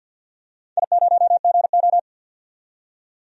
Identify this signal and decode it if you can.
Morse code, heard as a keyed tone, about 47 wpm, 700 Hz. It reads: I0GO
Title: DXCC-100 Call Signs - 4 Characters - 50wpm